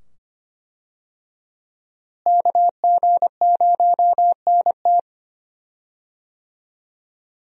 Morse code, heard as a keyed tone, about 25 wpm, 700 Hz. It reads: KG0NT
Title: USA Call Signs - 5 Characters - Repeated 2x - 25wpm